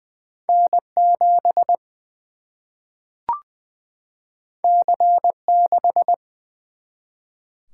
Morse code, heard as a keyed tone, about 20 wpm, 700 Hz. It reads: N 7 C 6
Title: Single-Letter Single-Number 20wpm